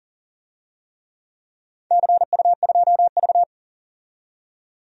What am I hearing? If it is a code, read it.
Morse code, heard as a keyed tone, about 40 wpm, 700 Hz. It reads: CU2V
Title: DXCC-100 Call Signs - 4 Characters - 40wpm